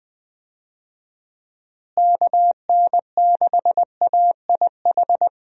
Morse code, heard as a keyed tone, about 20 wpm, 700 Hz. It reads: KN6AIH
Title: USA Call Signs - 6 Characters - Repeated 2x - 20wpm